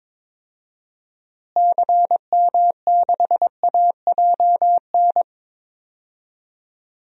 Morse code, heard as a keyed tone, about 21 wpm, 700 Hz. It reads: CM6AJN
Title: DXCC-100 Call Signs - 6 Characters - Repeated 3x - 22wpm